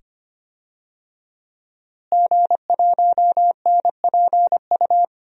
Morse code, heard as a keyed tone, about 25 wpm, 700 Hz. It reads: G1NPU
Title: DXCC-100 Call Signs - 5 Characters - Repeated 2x - 25wpm